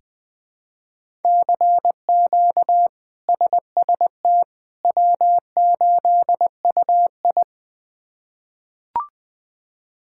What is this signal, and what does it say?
Morse code, heard as a keyed tone, about 21 wpm, 700 Hz. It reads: CQ SST W8UI
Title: K1USN SST QSO Practice - 20wpm